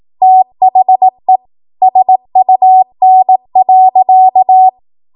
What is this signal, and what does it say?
Morse code, heard as a keyed tone, about 18 wpm, 750 Hz. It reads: THE SUN.
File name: Morse / 18WPM